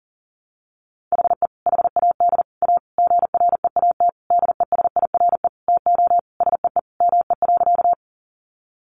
Morse code, heard as a keyed tone, about 40 wpm, 700 Hz. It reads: HE HAD A GREAT DESIRE TO SEE ME.
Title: Sentences from Top 1100 Words 40wpm